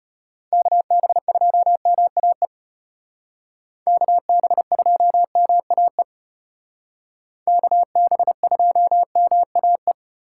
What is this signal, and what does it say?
Morse code, heard as a keyed tone, about 33 wpm, 700 Hz. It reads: KB2MAE KB2MAE KB2MAE